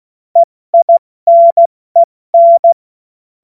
Morse code, heard as a keyed tone, about 16 wpm, 680 Hz. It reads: EINEN